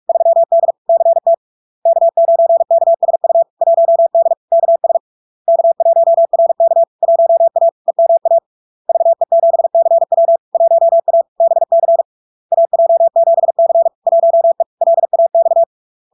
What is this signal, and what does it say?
Morse code, heard as a keyed tone, about 46 wpm, 670 Hz. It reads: 3D CT K9KSU 1D KS K1RK 1A EMA VE7CW 1A BC AJ7C 1E LAX
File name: LARK-FD-2022-Log-in-CW-45wpm-670-Hz